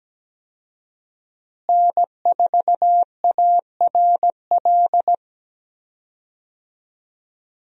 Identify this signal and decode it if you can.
Morse code, heard as a keyed tone, about 17 wpm, 700 Hz. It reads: N4ARL
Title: USA Call Signs - 5 Characters - Repeated 2x - 17wpm